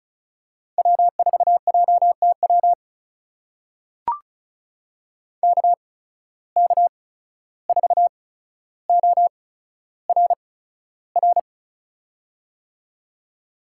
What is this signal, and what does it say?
Morse code, heard as a keyed tone, about 36 wpm, spaced as for 14 wpm, 700 Hz. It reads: W4JTW KK4ORR
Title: USA Call Signs - 8x Character Spacing - 35wpm